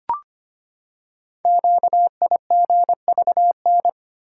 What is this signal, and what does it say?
Morse code, heard as a keyed tone, about 25 wpm, 690 Hz. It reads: QIGVN
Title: Code Groups - 5 Letters - 25wpm